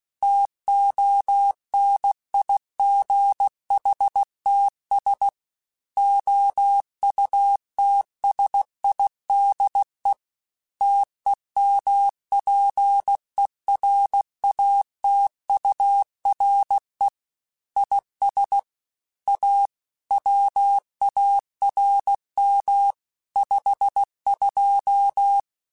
Morse code, heard as a keyed tone, about 16 wpm, 800 Hz. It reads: TONIGHTS OUTSIDE TEMPERATURE IS A WARM 52
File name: Weather Sets / Weather Sets-mp3s / Weather 4 10 @15